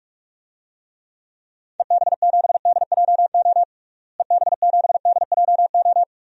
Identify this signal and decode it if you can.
Morse code, heard as a keyed tone, about 46 wpm, 700 Hz. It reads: EB7DJO EB7DJO
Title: DXCC-100 Call Signs - 6 Characters - Repeated 2x - 45wpm